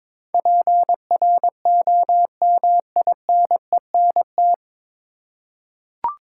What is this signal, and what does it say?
Morse code, heard as a keyed tone, about 22 wpm, 700 Hz. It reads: PROMINENT
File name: 9 Letter Words - First 3 Letters Given - 22wpm